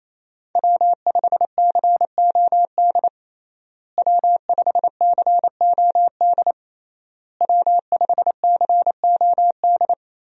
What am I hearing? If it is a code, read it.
Morse code, heard as a keyed tone, about 27 wpm, 700 Hz. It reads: W5COD W5COD W5COD
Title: USA Call Signs - 5 Characters - Repeated 3x - 28wpm